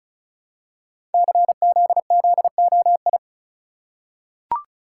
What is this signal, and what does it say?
Morse code, heard as a keyed tone, about 36 wpm, 700 Hz. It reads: CZZOI
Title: Code Groups - 5 Letters - 35wpm